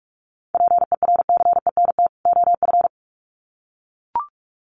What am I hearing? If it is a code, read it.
Morse code, heard as a keyed tone, about 44 wpm, 700 Hz. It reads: PERCENT OF
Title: Sets of 2 Words 45wpm